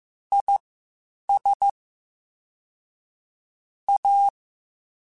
Morse code, heard as a keyed tone, about 15 wpm, spaced as for 8 wpm, 800 Hz. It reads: IS A
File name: Spaced Characters / Spaced Characters-mp3s / Weather 4 10 sp @13